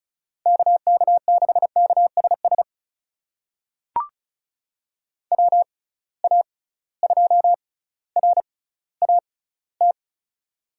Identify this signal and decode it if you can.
Morse code, heard as a keyed tone, about 36 wpm, spaced as for 18 wpm, 690 Hz. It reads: KK6KSS WA2RAT
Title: USA Call Signs - 6 Characters - 6x Character Spacing - 35wpm